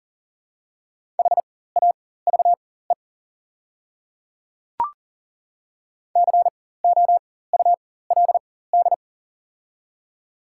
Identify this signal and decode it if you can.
Morse code, heard as a keyed tone, about 38 wpm, spaced as for 18 wpm, 700 Hz. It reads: HAVE COULD
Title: 4x Character Spacing - Top 100 Words - 40wpm